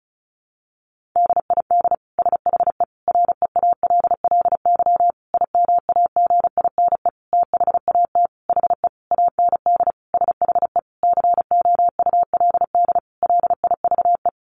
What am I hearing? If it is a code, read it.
Morse code, heard as a keyed tone, about 34 wpm, 700 Hz. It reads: DID SHE REALLY IMAGINE THAT HE AND SHE COULD LIVE
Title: Sentences from Top 1400 Words 35wpm